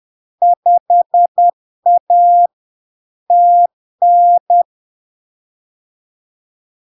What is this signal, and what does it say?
Morse code, heard as a keyed tone, about 10 wpm, 700 Hz. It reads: 5A TN
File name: LARK-FD-2022-Log-in-CW-10wpm-700-Hz